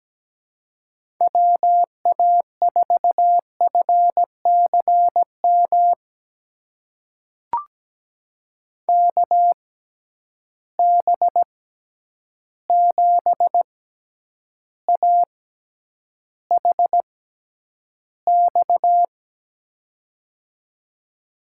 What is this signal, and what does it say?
Morse code, heard as a keyed tone, about 17 wpm, spaced as for 9 wpm, 700 Hz. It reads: WA4FCM KB7AHX
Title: USA Call Signs - 6x Character Spacing - 17wpm